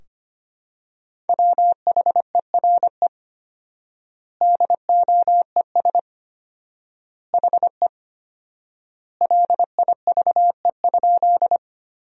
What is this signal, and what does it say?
Morse code, heard as a keyed tone, about 25 wpm, 700 Hz. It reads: WHERE DOES HE LIVE?
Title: 4x Word-Spacing - Sentences from Top 200 Words - 25wpm